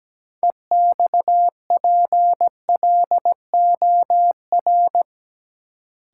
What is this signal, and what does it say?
Morse code, heard as a keyed tone, about 17 wpm, 700 Hz. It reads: EXPLOR
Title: Sentences from Top 1100 Words - New Word Drill - 17wpm